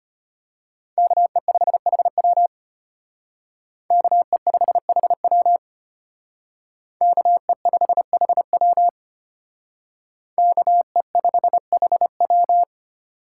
Morse code, heard as a keyed tone, about 32 wpm, 700 Hz. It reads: KE5HW KE5HW KE5HW KE5HW